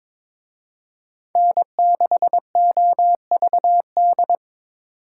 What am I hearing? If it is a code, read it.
Morse code, heard as a keyed tone, about 25 wpm, 700 Hz. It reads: N6OVD
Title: Speed Racing - USA Call Signs 22wpm